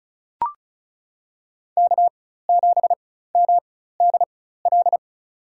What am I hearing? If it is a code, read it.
Morse code, heard as a keyed tone, about 36 wpm, spaced as for 16 wpm, 700 Hz. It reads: K7MDL
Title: USA Call Signs - 5 Characters - 4x Character Spacing - 35wpm